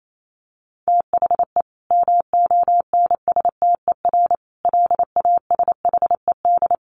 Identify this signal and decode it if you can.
Morse code, heard as a keyed tone, about 28 wpm, 700 Hz. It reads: THE MONSTER LASHED
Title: A Sound Of Thunder 28wpm